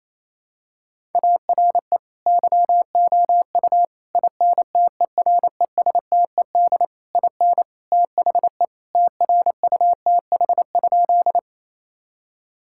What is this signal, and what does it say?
Morse code, heard as a keyed tone, about 28 wpm, 700 Hz. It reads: ARE YOU INTERESTED IN THE TRUTH?
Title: Sentences from Top 1000 Words - Review Part 6 - 28wpm